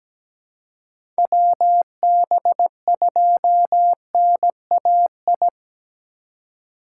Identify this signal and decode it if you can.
Morse code, heard as a keyed tone, about 17 wpm, 700 Hz. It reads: WB2NAI
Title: USA Call Signs - 2x Character Spacing - 17wpm